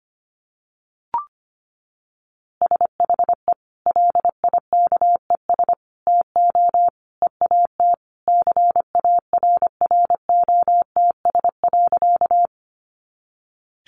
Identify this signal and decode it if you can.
Morse code, heard as a keyed tone, about 25 wpm, 700 Hz. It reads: SHE LIKES TO EAT CARROTS.
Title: Easy Phrases 25wpm